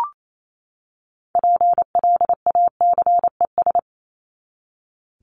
Morse code, heard as a keyed tone, about 28 wpm, 700 Hz. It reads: PLACES
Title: ES:S Suffix - Top 100 Words with Base Words 28wpm